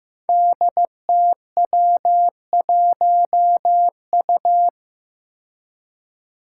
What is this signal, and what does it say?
Morse code, heard as a keyed tone, about 15 wpm, 700 Hz. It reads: DTW1U
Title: Five Challenging Letter-Number Combinations 15wpm